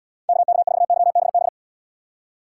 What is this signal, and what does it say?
Morse code, heard as a keyed tone, about 79 wpm, 700 Hz. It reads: KK4YBB